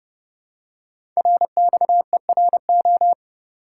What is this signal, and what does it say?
Morse code, heard as a keyed tone, about 30 wpm, 700 Hz. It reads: RXERO
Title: Code Groups - 5 Letters - 30wpm